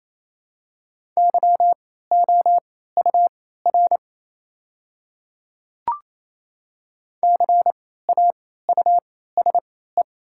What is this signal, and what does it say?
Morse code, heard as a keyed tone, about 28 wpm, spaced as for 16 wpm, 700 Hz. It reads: YOUR CAUSE
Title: 3x Character Spacing - Top 200 Words - 28wpm